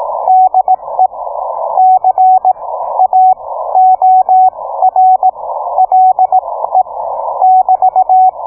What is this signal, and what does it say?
Morse code, heard as a keyed tone, about 18 wpm, spaced as for 15 wpm, 750 Hz. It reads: DE C A OR LE <BT>